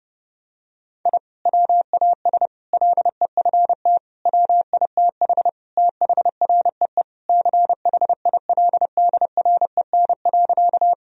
Morse code, heard as a keyed tone, about 30 wpm, 700 Hz. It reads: I WAS LEFT WITH THREE CHILDREN.
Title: Sentences from Top 1000 Words - Review Part 1 - 30wpm